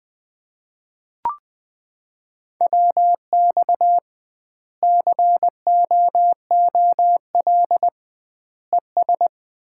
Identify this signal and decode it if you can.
Morse code, heard as a keyed tone, about 21 wpm, 700 Hz. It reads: WX COOL ES
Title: Common QSO Phrases - 2x Word-Spacing - 20wpm